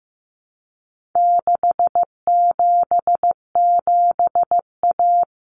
Morse code, heard as a keyed tone, about 15 wpm, 700 Hz. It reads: 677A
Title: Common Cut Numbers - 4-Digit 15wpm